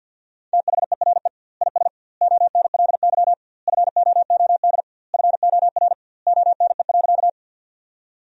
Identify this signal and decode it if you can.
Morse code, heard as a keyed tone, about 50 wpm, 700 Hz. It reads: THERE IS ONLY FOOD FOR ONE.
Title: 2x Word-Spacing - Sentences from Top 300 Words - 50wpm